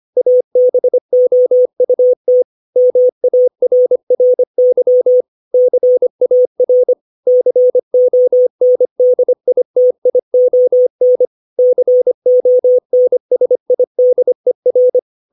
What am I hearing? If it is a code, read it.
Morse code, heard as a keyed tone, about 25 wpm, 500 Hz. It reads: ABOUT MARRY CAR CONDITION CONSIDER
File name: 632 words / 632 Words 25WPM